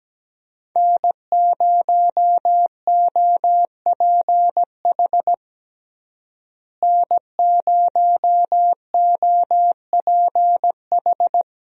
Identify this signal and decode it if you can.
Morse code, heard as a keyed tone, about 17 wpm, 700 Hz. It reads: N0OPH N0OPH
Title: USA Call Signs - 5 Characters - Repeated 3x - 17wpm